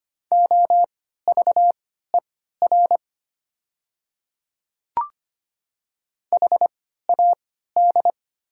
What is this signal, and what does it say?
Morse code, heard as a keyed tone, about 25 wpm, spaced as for 14 wpm, 700 Hz. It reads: OVER HAD
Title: 3x Character Spacing - Top 500 Words - 25wpm